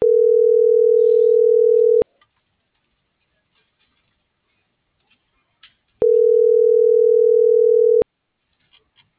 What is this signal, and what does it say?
Ringback tone (North America), 440 + 480 Hz.